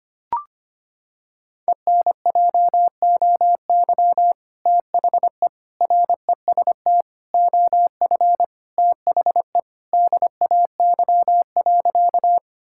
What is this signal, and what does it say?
Morse code, heard as a keyed tone, about 25 wpm, 700 Hz. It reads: ENJOY THE REST OF THE DAY.
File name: practice / 25wpm / Sentences with 2-Word Phrases - 25wpm